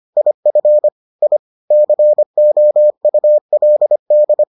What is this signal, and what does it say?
Morse code, heard as a keyed tone, about 25 wpm, 600 Hz. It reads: IF I COULD